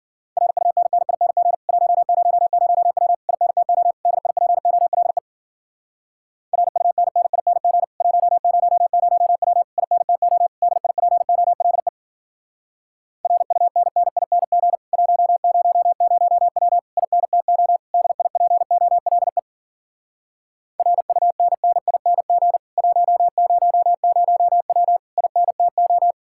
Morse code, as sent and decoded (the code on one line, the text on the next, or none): .-. ..- -. -. .. -. --. / .---- ----- ----- .-- / .. -. - --- / -.. .. .--. --- .-.. / .-. ..- -. -. .. -. --. / .---- ----- ----- .-- / .. -. - --- / -.. .. .--. --- .-.. / .-. ..- -. -. .. -. --. / .---- ----- ----- .-- / .. -. - --- / -.. .. .--. --- .-.. / .-. ..- -. -. .. -. --. / .---- ----- ----- .-- / .. -. - ---
RUNNING 100W INTO DIPOLE RUNNING 100W INTO DIPOLE RUNNING 100W INTO DIPOLE RUNNING 100W INTO